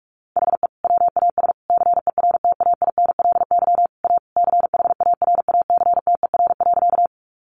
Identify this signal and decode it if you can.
Morse code, heard as a keyed tone, about 44 wpm, 700 Hz. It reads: HE WAS CERTAINLY A CHARACTER.
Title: Sentences from Top 1200 Words 45wpm